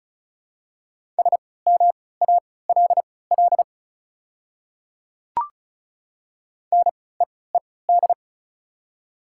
Morse code, heard as a keyed tone, about 36 wpm, spaced as for 20 wpm, 700 Hz. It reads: SMALL NEED
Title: 3x Character Spacing - Top 200 Words - 35wpm